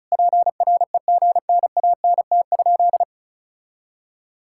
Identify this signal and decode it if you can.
Morse code, heard as a keyed tone, about 34 wpm, 700 Hz. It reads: PREGNANT?
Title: Sentences from Top 1000 Words - Review Part 6 - 35wpm